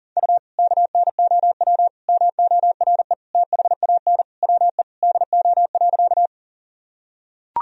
Morse code, heard as a keyed tone, about 40 wpm, 700 Hz. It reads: U KNOW MORE THAN WE DO.
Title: Sentences from Top 100 Words and Abbreviations 40wpm